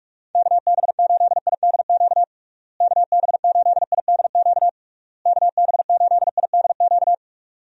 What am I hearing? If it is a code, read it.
Morse code, heard as a keyed tone, about 44 wpm, 700 Hz. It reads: KB8IDQ KB8IDQ KB8IDQ
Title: USA Call Signs - 6 Characters - Repeated 3x - 45wpm